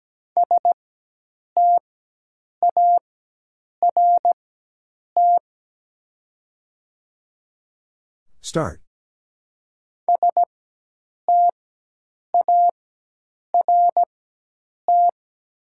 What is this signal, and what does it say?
Morse code, heard as a keyed tone, about 17 wpm, spaced as for 8 wpm, 700 Hz. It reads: START START
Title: 4x Character Spacing - Top 300 Words - 17wpm